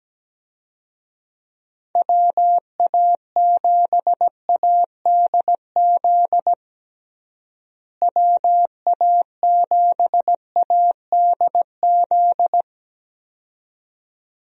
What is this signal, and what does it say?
Morse code, heard as a keyed tone, about 17 wpm, 700 Hz. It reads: WA7ADZ WA7ADZ
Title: USA Call Signs - 6 Characters - Repeated 2x - 17wpm